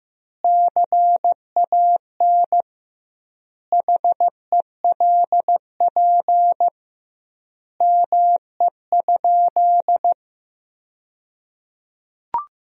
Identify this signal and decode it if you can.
Morse code, heard as a keyed tone, about 15 wpm, 700 Hz. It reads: CAN HELP ME?
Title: 2x Word-Spacing - Sentences from Top 200 Words - 15wpm